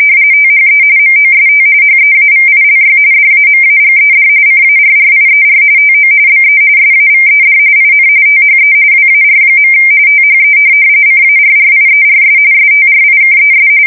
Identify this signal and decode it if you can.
RTTY, about 45.45 baud. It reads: THE MOST HEAVILY DAMAGED CARRIER TO BE SAVED IN THE WAR. THE CARRIERS STRIKES
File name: fsk170 / USS-New-Jersey